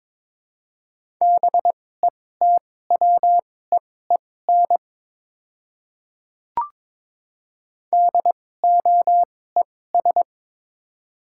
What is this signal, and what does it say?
Morse code, heard as a keyed tone, about 22 wpm, spaced as for 16 wpm, 700 Hz. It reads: BETWEEN DOES
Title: 2x Character Spacing - Top 300 Words - 22wpm